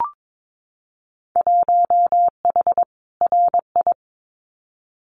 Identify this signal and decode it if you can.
Morse code, heard as a keyed tone, about 22 wpm, 700 Hz. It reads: 1H RI
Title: Winter Field Day Exchanges 22wpm